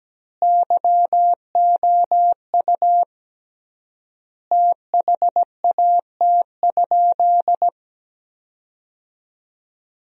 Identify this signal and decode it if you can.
Morse code, heard as a keyed tone, about 17 wpm, 700 Hz. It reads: YOU THAT?
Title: 3x Word-Spacing - Sentences from Top 400 Words - 17wpm